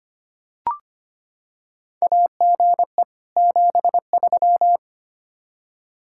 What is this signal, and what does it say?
Morse code, heard as a keyed tone, about 25 wpm, 700 Hz. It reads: AGE 73
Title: Practice QSO - Exchange 3 - More Info - 25wpm